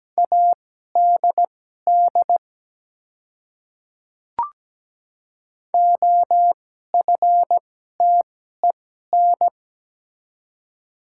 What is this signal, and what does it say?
Morse code, heard as a keyed tone, about 17 wpm, spaced as for 12 wpm, 700 Hz. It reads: ADD OFTEN
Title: 2x Character Spacing - Top 300 Words - 17wpm